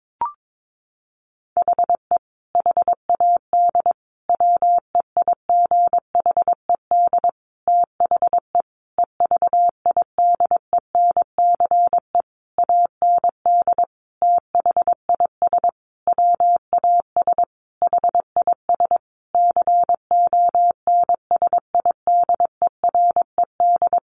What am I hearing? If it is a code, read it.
Morse code, heard as a keyed tone, about 22 wpm, 700 Hz. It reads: HE HAD WEIGHED THE EVIDENCE AND THIS WAS HIS CONSIDERED
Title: A Sound Of Thunder 22wpm